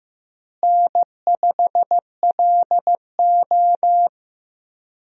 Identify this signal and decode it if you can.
Morse code, heard as a keyed tone, about 15 wpm, 700 Hz. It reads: N5LO
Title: USA Call Signs - 4 Characters - Repeated 2x - 15wpm